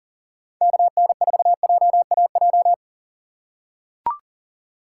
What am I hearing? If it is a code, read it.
Morse code, heard as a keyed tone, about 40 wpm, 700 Hz. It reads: KN4JAJ
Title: USA Call Signs - 6 Characters - 2x Character Spacing - 40wpm